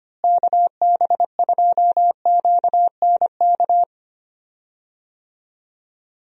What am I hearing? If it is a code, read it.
Morse code, heard as a keyed tone, about 25 wpm, 700 Hz. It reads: KB2QNK
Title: USA Call Signs - 6 Characters - Repeated 3x - 25wpm